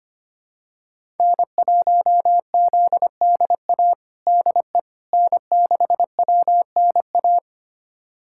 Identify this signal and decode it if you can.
Morse code, heard as a keyed tone, about 25 wpm, 700 Hz. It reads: N1ZDA DE N6WNA
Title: USA Callsign DE Callsign - 5 Characters - 25wpm